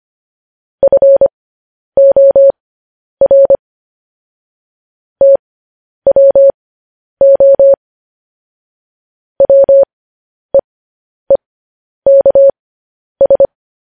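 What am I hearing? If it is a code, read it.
Morse code, heard as a keyed tone, about 25 wpm, spaced as for 10 wpm, 550 Hz. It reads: FOR TWO WEEKS
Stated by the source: Morse code news headlines sent at 10 words per minute.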